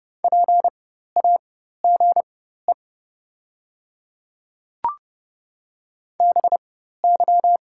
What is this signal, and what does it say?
Morse code, heard as a keyed tone, about 30 wpm, spaced as for 14 wpm, 700 Hz. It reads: PAGE BY
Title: 4x Character Spacing - Top 300 Words - 30wpm